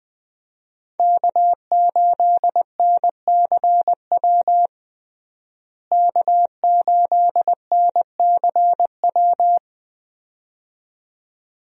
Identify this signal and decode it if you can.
Morse code, heard as a keyed tone, about 20 wpm, 700 Hz. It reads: K8NCW K8NCW
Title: USA Call Signs - 5 Characters - Repeated 2x - 20wpm